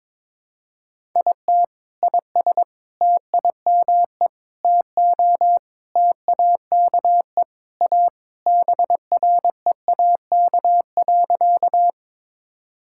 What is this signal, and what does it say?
Morse code, heard as a keyed tone, about 22 wpm, 700 Hz. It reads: IT IS TIME TO TAKE A BREAK.
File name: practice / 22wpm / Sentences with 4-Word Phrases - 22wpm